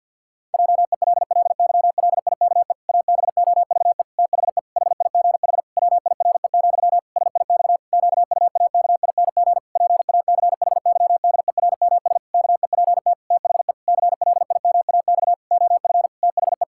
Morse code, heard as a keyed tone, about 50 wpm, 700 Hz. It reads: JELLYLIKE ABOVE THE HIGH WIRE, SIX QUAKING PACHYDERMS KEPT THE CLIMAX OF THE
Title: Pangram Challenge v2 - 50wpm